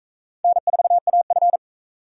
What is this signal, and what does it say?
Morse code, heard as a keyed tone, about 38 wpm, 700 Hz. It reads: N4AF